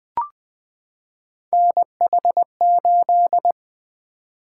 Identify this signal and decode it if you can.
Morse code, heard as a keyed tone, about 20 wpm, 700 Hz. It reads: NH8
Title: Three Letter-Number Combinations - 20wpm